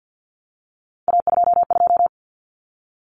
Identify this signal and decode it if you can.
Morse code, heard as a keyed tone, about 50 wpm, 700 Hz. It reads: A22
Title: Common Cut Numbers - 3-Digit 50wpm